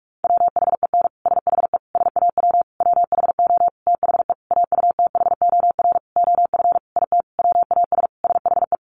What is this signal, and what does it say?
Morse code, heard as a keyed tone, about 44 wpm, 700 Hz. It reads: WHEN SHE SAW WHO THE AUTHOR OF IT WAS SHE
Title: Sentences from Top 1200 Words 45wpm